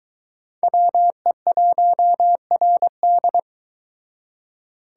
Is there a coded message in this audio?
Yes: Morse code from 0.6 to 3.4 s, heard as a keyed tone, about 22 wpm, 700 Hz. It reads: WE1RD